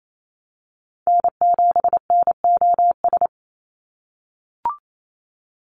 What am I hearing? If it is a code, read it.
Morse code, heard as a keyed tone, about 28 wpm, 700 Hz. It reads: N7NOS
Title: USA Call Signs 28wpm